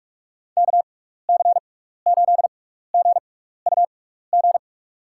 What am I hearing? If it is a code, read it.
Morse code, heard as a keyed tone, about 45 wpm, spaced as for 15 wpm, 700 Hz. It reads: KC8GUG
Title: USA Call Signs - 6 Characters - 6x Character Spacing - 45wpm